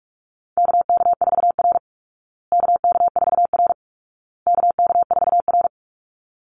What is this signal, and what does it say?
Morse code, heard as a keyed tone, about 44 wpm, 700 Hz. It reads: KK4R KK4R KK4R
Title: Call Signs Repeated 2x - 45wpm